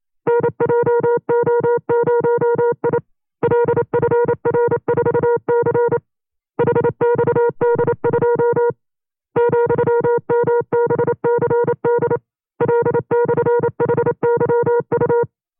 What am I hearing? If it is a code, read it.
Morse code, heard as a keyed tone, about 28 wpm, 480 Hz. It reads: NJO0I LFR4C HXD2 ,MBCD L/HYU
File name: 25WPM